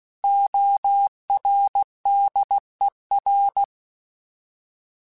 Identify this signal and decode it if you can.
Morse code, heard as a keyed tone, about 16 wpm, 800 Hz. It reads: ORDER
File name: Alphabetical / Alphabetical mp3s / Os @15